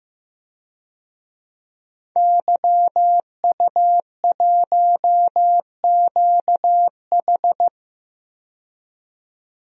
Morse code, heard as a keyed tone, about 16 wpm, 700 Hz. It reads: YU1QH
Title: DXCC-100 Call Signs - 5 Characters - 15wpm